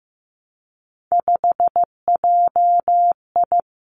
Morse code, heard as a keyed tone, about 15 wpm, 700 Hz. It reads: 5JI
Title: Three Challenging Letter-Number Combinations 15wpm